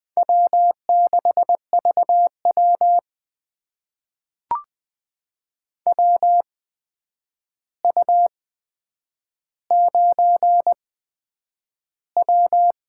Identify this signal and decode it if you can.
Morse code, heard as a keyed tone, about 20 wpm, spaced as for 9 wpm, 700 Hz. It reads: W6VW WU9W
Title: USA Call Signs - 4 Characters - 8x Character Spacing - 20wpm